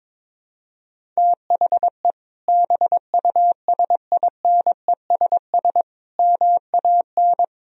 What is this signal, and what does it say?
Morse code, heard as a keyed tone, about 22 wpm, 700 Hz. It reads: THE BUSINESS MAN
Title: Sentences from Top 1000 Words - Review Part 1 - 22wpm